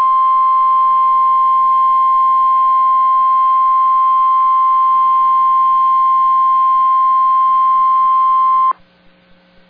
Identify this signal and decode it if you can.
Emergency Alert System (SAME) tones (the 1050 Hz attention tone). It is heard from 0.3 to 8.7 s.